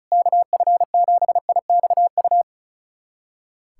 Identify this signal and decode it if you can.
Morse code, heard as a keyed tone, about 36 wpm, 700 Hz. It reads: KF7IXU
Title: USA Callsign DE Callsign - 6 Characters - 35wpm